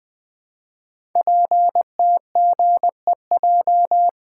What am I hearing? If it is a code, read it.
Morse code, heard as a keyed tone, about 20 wpm, 700 Hz. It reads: PTGEJ
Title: Code Groups - 5 Letters - 20wpm